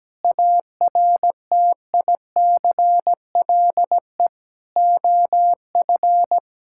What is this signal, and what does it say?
Morse code, heard as a keyed tone, about 17 wpm, 700 Hz. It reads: ARTICLE OF
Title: Sentences from Top 1000 Words - Review Part 5 - 17wpm